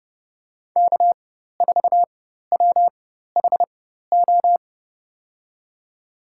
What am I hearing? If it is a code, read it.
Morse code, heard as a keyed tone, about 30 wpm, spaced as for 14 wpm, 700 Hz. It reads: K4WHO
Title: USA Call Signs - 4x Character Spacing - 30wpm